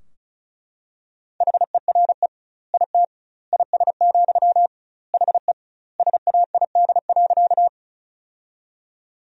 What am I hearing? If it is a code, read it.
Morse code, heard as a keyed tone, about 36 wpm, 700 Hz. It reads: HERE IT IS, HE SAID.
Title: 2x Word-Spacing - Sentences from Top 200 Words - 35wpm